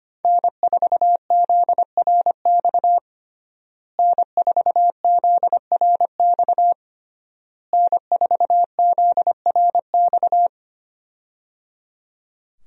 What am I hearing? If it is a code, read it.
Morse code, heard as a keyed tone, about 25 wpm, 700 Hz. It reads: N4ZRX N4ZRX N4ZRX
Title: USA Call Signs - 5 Characters - Repeated 3x - 25wpm